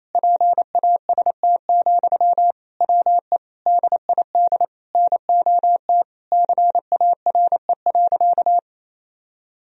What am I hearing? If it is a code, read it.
Morse code, heard as a keyed tone, about 28 wpm, 700 Hz. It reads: PAST, WE DID NOT CARE.
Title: Sentences from Top 1000 Words - Review Part 3 - 28wpm